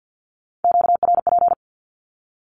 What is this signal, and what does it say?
Morse code, heard as a keyed tone, about 50 wpm, 700 Hz. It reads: QRP
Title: Q-Codes Repeated 3x 50wpm